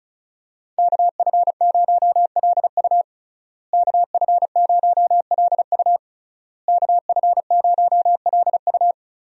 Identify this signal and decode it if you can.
Morse code, heard as a keyed tone, about 36 wpm, 700 Hz. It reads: KF0LU KF0LU KF0LU
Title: USA Call Signs - 5 Characters - Repeated 3x - 35wpm